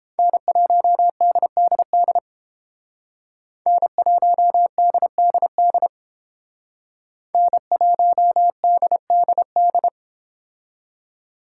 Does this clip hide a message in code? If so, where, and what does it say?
Yes: Morse code from 0.2 to 9.9 s, heard as a keyed tone, about 31 wpm, 700 Hz. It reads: N1DDD N1DDD N1DDD